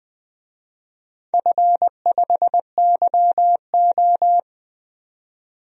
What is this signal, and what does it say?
Morse code, heard as a keyed tone, about 20 wpm, 700 Hz. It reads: F5YO
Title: DXCC-100 Call Signs - 4 Characters - 20wpm